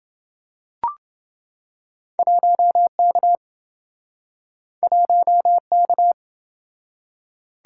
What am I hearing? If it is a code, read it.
Morse code, heard as a keyed tone, about 29 wpm, 700 Hz. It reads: 1K 1K